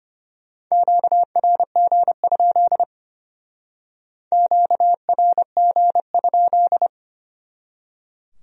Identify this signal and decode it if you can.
Morse code, heard as a keyed tone, about 32 wpm, 700 Hz. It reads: QRG? QRG?